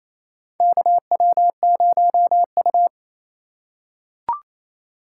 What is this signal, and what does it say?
Morse code, heard as a keyed tone, about 28 wpm, 700 Hz. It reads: KW0U
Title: USA Call Signs - 4 Characters - 4x Character Spacing - 28wpm